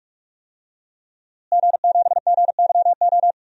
Morse code, heard as a keyed tone, about 46 wpm, 700 Hz. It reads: G7GYO
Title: DXCC-100 Call Signs - 5 Characters - Repeated 2x - 45wpm